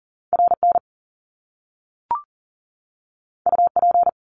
Morse code, heard as a keyed tone, about 38 wpm, 700 Hz. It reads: RN UP
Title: Top Two Letter Combinations in Words 40wpm